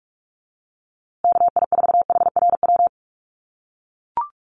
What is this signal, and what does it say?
Morse code, heard as a keyed tone, about 44 wpm, 700 Hz. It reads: KI4HRW
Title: USA Call Signs Repeated 3x 45wpm